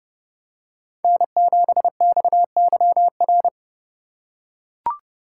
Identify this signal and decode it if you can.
Morse code, heard as a keyed tone, about 27 wpm, 700 Hz. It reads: N7XYR